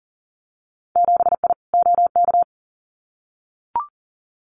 Morse code, heard as a keyed tone, about 42 wpm, 700 Hz. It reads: 7I OK
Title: Winter Field Day Exchanges Encore 40wpm